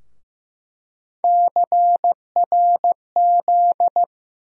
Morse code, heard as a keyed tone, about 15 wpm, 700 Hz. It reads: CRZ
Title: USA Call Sign Postfixes - 3 Letters 15wpm